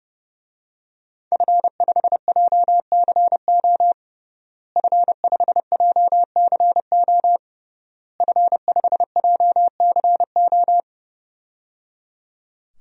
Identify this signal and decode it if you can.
Morse code, heard as a keyed tone, about 30 wpm, 700 Hz. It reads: F5JCO F5JCO F5JCO
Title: DXCC-100 Call Signs - 5 Characters - Repeated 3x - 30wpm